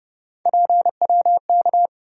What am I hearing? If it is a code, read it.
Morse code, heard as a keyed tone, about 30 wpm, 700 Hz. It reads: PWK
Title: USA Call Sign Postfixes - 3 Letters 30wpm